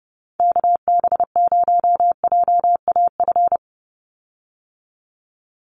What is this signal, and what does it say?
Morse code, heard as a keyed tone, about 30 wpm, 700 Hz. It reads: KB0JAF
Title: USA Call Signs 30wpm